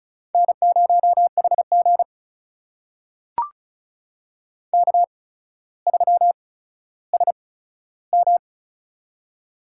Morse code, heard as a keyed tone, about 36 wpm, spaced as for 15 wpm, 700 Hz. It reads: N0HG K3SM
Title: USA Call Signs - 4 Characters - 8x Character Spacing - 35wpm